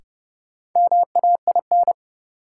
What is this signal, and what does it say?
Morse code, heard as a keyed tone, about 31 wpm, 700 Hz. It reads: MAIN
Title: Sentences from Top 1000 Words - Review Part 2 - New Word Drill - 30wpm